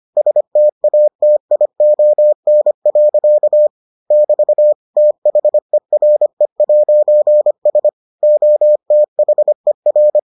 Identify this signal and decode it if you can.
Morse code, heard as a keyed tone, about 25 wpm, 600 Hz. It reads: STATION. <BT> THERE'S OTHER